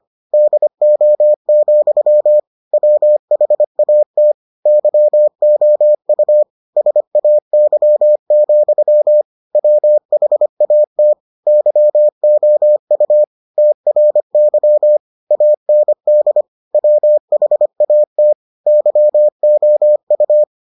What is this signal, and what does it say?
Morse code, heard as a keyed tone, about 25 wpm, 600 Hz. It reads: DO, WHAT YOU SAY, WHAT YOU TRY AND WHAT YOU